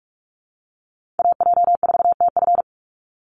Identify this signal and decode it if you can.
Morse code, heard as a keyed tone, about 44 wpm, 700 Hz. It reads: AJ4TF
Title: USA Call Signs 45wpm